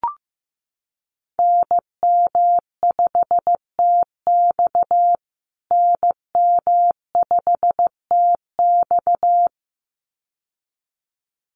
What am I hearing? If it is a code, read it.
Morse code, heard as a keyed tone, about 15 wpm, 700 Hz. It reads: NM5TX NM5TX
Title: USA Call Signs Repeated 2x 15wpm